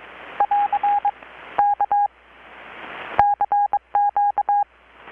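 Morse code, heard as a keyed tone, about 22 wpm, 820 Hz. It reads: <AR> K CQ
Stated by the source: Morse from KPH's Night of Nights XX maritime radiotelegraphy event, received in Chiapas, Mexico about 3,800 km away on a Kenwood R-600 with a grounded 10 m long-wire antenna.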